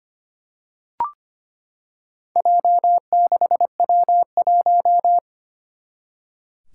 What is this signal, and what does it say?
Morse code, heard as a keyed tone, about 25 wpm, 700 Hz. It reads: J6W1
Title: Four Challenging Letter-Number Combinations 25wpm